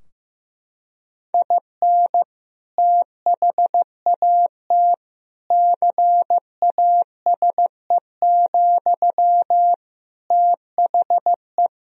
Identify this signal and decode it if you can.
Morse code, heard as a keyed tone, about 16 wpm, 700 Hz. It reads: IN THAT CASE, THE
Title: Sentences from Top 1000 Words - Review Part 2 - 15wpm